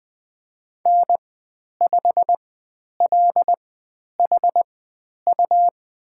Morse code, heard as a keyed tone, about 20 wpm, spaced as for 10 wpm, 700 Hz. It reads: N5LHU
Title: 102 Lesson 17g - Letter u - Call Signs 20wpm